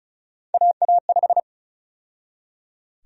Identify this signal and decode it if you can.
Morse code, heard as a keyed tone, about 36 wpm, 700 Hz. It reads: AA5
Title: USA Call Sign Prefixes - 35wpm